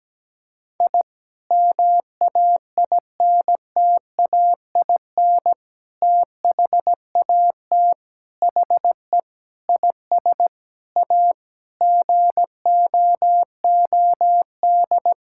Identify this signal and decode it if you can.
Morse code, heard as a keyed tone, about 17 wpm, 700 Hz. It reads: I MAINTAIN THAT HE IS A GOOD
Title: Sentences from Top 1000 Words - Review Part 5 - 17wpm